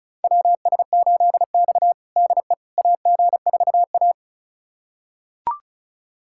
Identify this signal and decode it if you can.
Morse code, heard as a keyed tone, about 34 wpm, 700 Hz. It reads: WS8X DE AG4A
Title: USA Callsign DE Callsign - 4 Characters - 35wpm